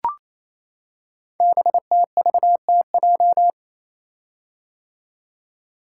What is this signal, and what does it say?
Morse code, heard as a keyed tone, about 29 wpm, 700 Hz. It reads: BTVTJ
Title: Code Groups - 5 Letters - 28wpm